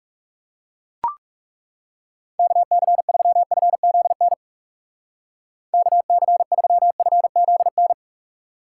Morse code, heard as a keyed tone, about 37 wpm, 700 Hz. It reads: KC3FZN KC3FZN
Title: Speed Racing - USA 2x3 Call Signs 30wpm